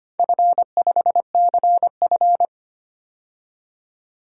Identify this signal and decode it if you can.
Morse code, heard as a keyed tone, about 25 wpm, 700 Hz. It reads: F5CF
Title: DXCC-100 Call Signs - 4 Characters - 25wpm